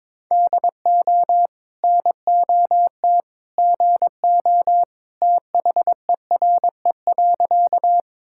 Morse code, heard as a keyed tone, about 22 wpm, 700 Hz. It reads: DO NOT GO THERE.
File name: practice / 22wpm / Sentences with 2-Word Phrases - 22wpm